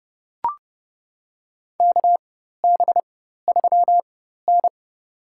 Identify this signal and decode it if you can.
Morse code, heard as a keyed tone, about 30 wpm, spaced as for 14 wpm, 700 Hz. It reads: KB3N
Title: USA Call Signs - 4 Characters - 4x Character Spacing - 30wpm